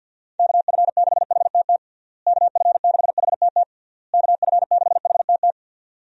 Morse code, heard as a keyed tone, about 52 wpm, 700 Hz. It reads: KF6HTT KF6HTT KF6HTT
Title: USA Call Signs - 6 Characters - Repeated 3x - 50wpm